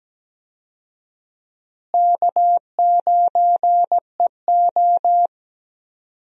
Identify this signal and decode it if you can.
Morse code, heard as a keyed tone, about 17 wpm, 700 Hz. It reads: K9EO
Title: USA Call Signs - 4 Characters - Repeated 3x - 17wpm